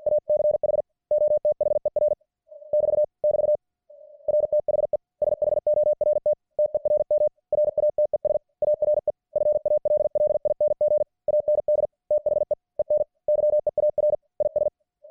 Morse code, heard as a keyed tone, about 50 wpm, 600 Hz. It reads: ACH OTHER <BT> <BT> RTHE SHORT TERM RATES ARE FALLING AND THE EN YEAR IS